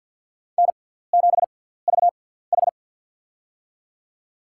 Morse code, heard as a keyed tone, about 47 wpm, spaced as for 17 wpm, 700 Hz. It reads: N7VH
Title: USA Call Signs - 4 Characters - 6x Character Spacing - 50wpm